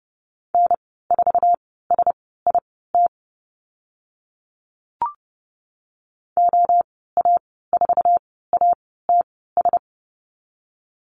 Morse code, heard as a keyed tone, about 30 wpm, spaced as for 17 wpm, 700 Hz. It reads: N4SIT OA4ATS
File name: course / 30:10wpm / 088 Lesson 15g - Number 5 - Call Signs 30wpm